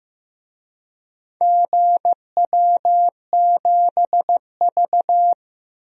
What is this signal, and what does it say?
Morse code, heard as a keyed tone, about 15 wpm, 700 Hz. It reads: GW7V
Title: DXCC-100 Call Signs - 4 Characters - 15wpm